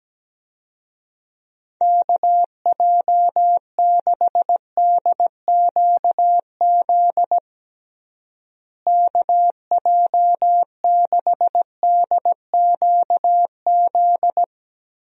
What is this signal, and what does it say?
Morse code, heard as a keyed tone, about 17 wpm, 700 Hz. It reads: KJ6DQZ KJ6DQZ
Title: USA Call Signs - 6 Characters - Repeated 2x - 17wpm